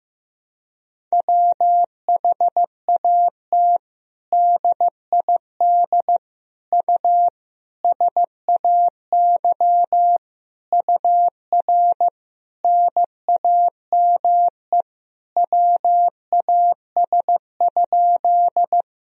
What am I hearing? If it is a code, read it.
Morse code, heard as a keyed tone, about 16 wpm, 700 Hz. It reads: WHAT DID U SAY UR NAME WAS?
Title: Sentences from Top 200 Words and Abbreviations 15wpm